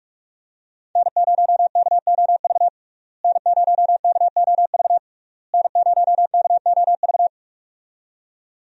Morse code, heard as a keyed tone, about 41 wpm, 700 Hz. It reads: N0KOV N0KOV N0KOV
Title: USA Call Signs - 5 Characters - Repeated 3x - 45wpm